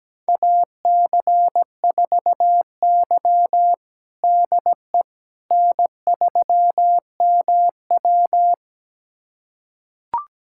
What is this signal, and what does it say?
Morse code, heard as a keyed tone, about 17 wpm, 700 Hz. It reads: AC4Y DE N3MW
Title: USA Callsign DE Callsign - 4 Characters - 17wpm